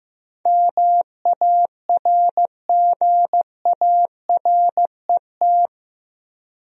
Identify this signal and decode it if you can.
Morse code, heard as a keyed tone, about 15 wpm, 700 Hz. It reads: MARGARET
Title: USA Common Names - 15wpm